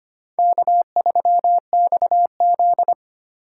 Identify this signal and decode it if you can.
Morse code, heard as a keyed tone, about 25 wpm, 700 Hz. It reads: K3XZ
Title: USA Call Signs - 4 Characters - 3x Character Spacing - 25wpm